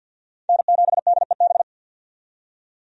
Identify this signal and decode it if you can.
Morse code, heard as a keyed tone, about 51 wpm, 700 Hz. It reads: N7DEB
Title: USA Call Signs - 5 Characters - 50wpm